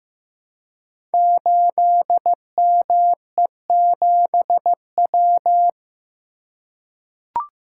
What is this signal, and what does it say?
Morse code, heard as a keyed tone, about 15 wpm, 700 Hz. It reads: 8ME7W
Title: Code Groups - 5 Letter:Numbers - 15wpm